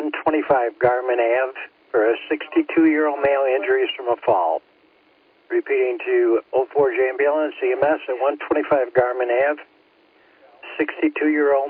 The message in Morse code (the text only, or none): none